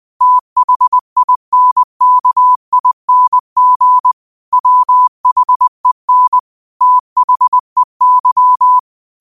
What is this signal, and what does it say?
Morse code, heard as a keyed tone, about 20 wpm, 1000 Hz. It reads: THINKING WHEN THEY
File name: qotdfiles / 20wpm